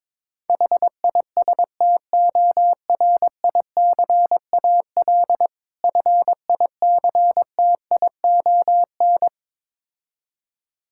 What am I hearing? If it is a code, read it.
Morse code, heard as a keyed tone, about 22 wpm, 690 Hz. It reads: HISTORICAL FICTION
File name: Difficult Words with Hints - 22wpm